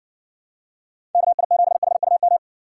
Morse code, heard as a keyed tone, about 64 wpm, 700 Hz. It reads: KI7HUM